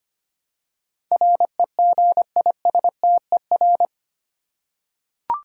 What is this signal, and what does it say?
Morse code, heard as a keyed tone, about 25 wpm, 700 Hz. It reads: REGISTER
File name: 8 Letter Words - First 3 Letters Given - 25wpm